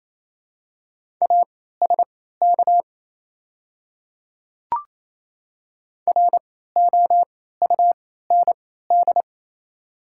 Morse code, heard as a keyed tone, about 28 wpm, spaced as for 16 wpm, 700 Hz. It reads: ASK ROUND
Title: 3x Character Spacing - Top 200 Words - 28wpm